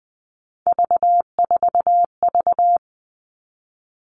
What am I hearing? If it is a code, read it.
Morse code, heard as a keyed tone, about 20 wpm, 700 Hz. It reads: V4V
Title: All Cut Numbers - 3-Digit 20wpm